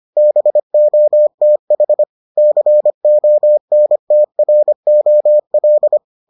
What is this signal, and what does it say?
Morse code, heard as a keyed tone, about 25 wpm, 600 Hz. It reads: BOTH CONTROL